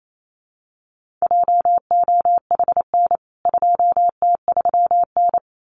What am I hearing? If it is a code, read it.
Morse code, heard as a keyed tone, about 28 wpm, 700 Hz. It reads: JOHN 2T3N
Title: CWT Exchange 28wpm